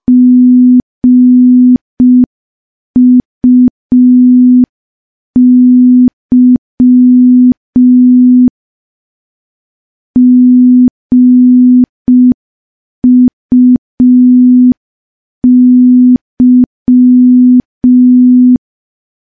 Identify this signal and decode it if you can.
Morse code, heard as a keyed tone, about 5 wpm, 250 Hz. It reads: GUY GUY
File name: morse / morse-training-05WPM